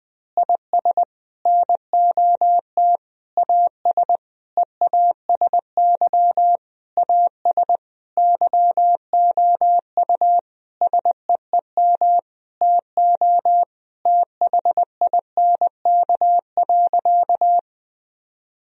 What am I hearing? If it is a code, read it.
Morse code, heard as a keyed tone, about 20 wpm, 700 Hz. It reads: IS NOT AS EASY AS YOU SEEM TO THINK.
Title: Sentences from Top 1000 Words - Review Part 1 - 20wpm